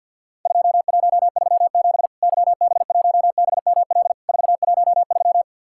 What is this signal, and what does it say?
Morse code, heard as a keyed tone, about 50 wpm, 700 Hz. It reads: 2137 YB1BML 413
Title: Worked All Europe DX Contest - QTC Exchange - 50wpm